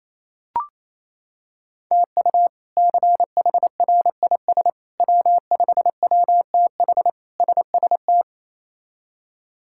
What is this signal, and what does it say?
Morse code, heard as a keyed tone, about 28 wpm, 700 Hz. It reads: TU CHRIS W5WTH SST
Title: K1USN SST QSO Practice - 28wpm